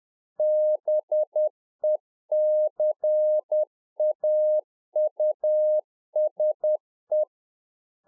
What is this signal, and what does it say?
Morse code, heard as a keyed tone, about 10 wpm, 610 Hz. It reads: BECAUSE
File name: 2011-04-mp3 / RICK JJ0MPI 10WPM